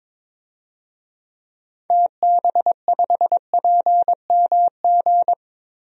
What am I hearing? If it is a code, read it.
Morse code, heard as a keyed tone, about 22 wpm, 700 Hz. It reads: TB5PMG
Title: DXCC-100 Call Signs - 6 Characters - Repeated 3x - 22wpm